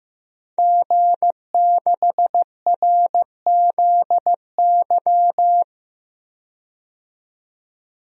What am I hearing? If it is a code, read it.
Morse code, heard as a keyed tone, about 15 wpm, 700 Hz. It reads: G6RZY
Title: DXCC-100 Call Signs - 5 Characters - 15wpm